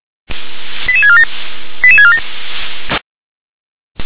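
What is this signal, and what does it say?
Five-tone selective call (ZVEI), heard as quick runs of five tones, tones of about 80 ms. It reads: 88647 88647
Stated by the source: ZVEI five-tone selective call received on an SDR.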